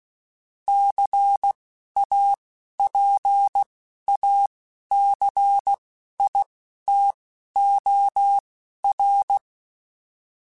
Morse code, heard as a keyed tone, about 16 wpm, spaced as for 11 wpm, 800 Hz. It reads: CAPACITOR
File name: Technical Subjects / Technical Subjects-mp3s / XC @10